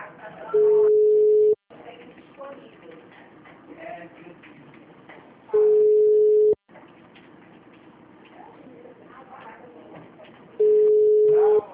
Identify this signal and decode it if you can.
Ringback tone (Europe, 425 Hz), with speech.
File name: out- -2030- - - .3304